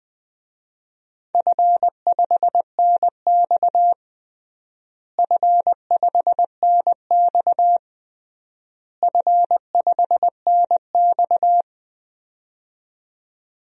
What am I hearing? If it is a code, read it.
Morse code, heard as a keyed tone, about 19 wpm, 700 Hz. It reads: F5NX F5NX F5NX
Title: DXCC-100 Call Signs - 4 Characters - Repeated 3x - 20wpm